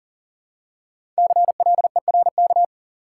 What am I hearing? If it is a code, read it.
Morse code, heard as a keyed tone, about 40 wpm, 700 Hz. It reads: CLERK
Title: Five Letter Words - 40wpm